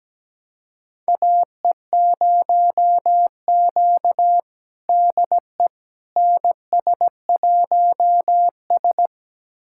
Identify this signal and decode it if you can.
Morse code, heard as a keyed tone, about 17 wpm, 700 Hz. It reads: AE0Q DE NS1S
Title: USA Callsign DE Callsign - 4 Characters - 17wpm